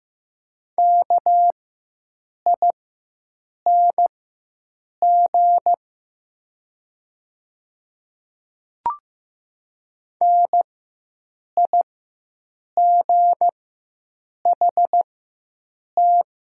Morse code, heard as a keyed tone, about 15 wpm, spaced as for 7 wpm, 700 Hz. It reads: KING NIGHT
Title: 4x Character Spacing - Top 500 Words - 15wpm